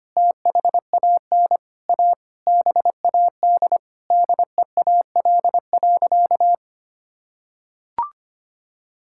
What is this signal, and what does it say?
Morse code, heard as a keyed tone, about 25 wpm, 700 Hz. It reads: THAN A BAD DEAL.
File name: practice / 25wpm / Sentences with 9-Word Phrases - 25wpm